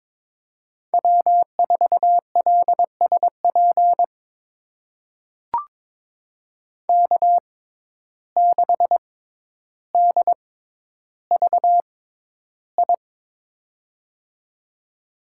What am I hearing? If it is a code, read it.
Morse code, heard as a keyed tone, about 22 wpm, spaced as for 11 wpm, 700 Hz. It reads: W4LSP K6DVI
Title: USA Call Signs - 5 Characters - 6x Character Spacing - 22wpm